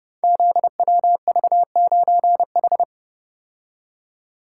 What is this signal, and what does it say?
Morse code, heard as a keyed tone, about 28 wpm, 700 Hz. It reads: ZWV9H
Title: Code Groups - 5 Letter:Numbers - 30wpm